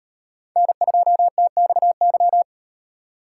Morse code, heard as a keyed tone, about 37 wpm, 700 Hz. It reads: N2TXY